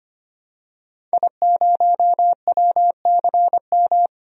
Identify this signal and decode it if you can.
Morse code, heard as a keyed tone, about 25 wpm, 700 Hz. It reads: I0WCM
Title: DXCC-100 Call Signs - 5 Characters - 25wpm